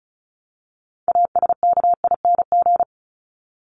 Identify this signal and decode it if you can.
Morse code, heard as a keyed tone, about 34 wpm, 700 Hz. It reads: ASKING
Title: ING Suffix - Top 100 Words with Base Words 35wpm